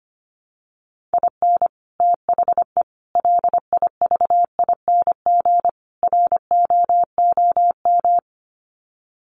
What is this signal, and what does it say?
Morse code, heard as a keyed tone, about 25 wpm, 700 Hz. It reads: IN THE LIVING ROOM
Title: Sets of 4 Words 25wpm